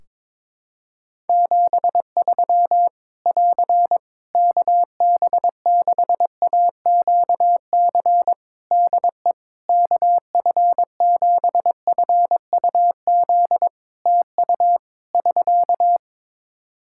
Morse code, heard as a keyed tone, about 22 wpm, 700 Hz. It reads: 73 <AR> KB6AQC DE KF7FUZ TU <SK>
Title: Practice QSO - Exchange 4 - Ending - 22wpm